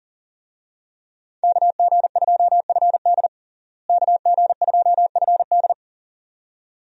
Morse code, heard as a keyed tone, about 40 wpm, 700 Hz. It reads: KG2FD KG2FD
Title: USA Call Signs - 5 Characters - Repeated 2x - 40wpm